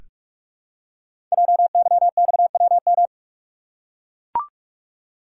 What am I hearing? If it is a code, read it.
Morse code, heard as a keyed tone, about 44 wpm, 690 Hz. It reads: JYXWM
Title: Code Groups - 5 Letters - 45wpm